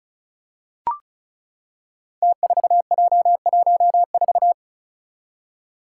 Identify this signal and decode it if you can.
Morse code, heard as a keyed tone, about 36 wpm, 700 Hz. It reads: T4J14
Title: Five Challenging Letter-Number Combinations 35wpm